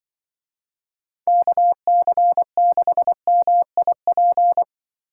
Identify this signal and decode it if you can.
Morse code, heard as a keyed tone, about 24 wpm, 700 Hz. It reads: KC6MIP
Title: Speed Racing - USA 2x3 Call Signs 20wpm